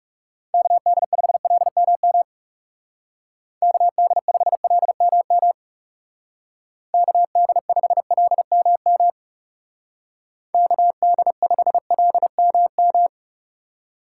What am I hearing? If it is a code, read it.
Morse code, heard as a keyed tone, about 39 wpm, 700 Hz. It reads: KD5LMM KD5LMM KD5LMM KD5LMM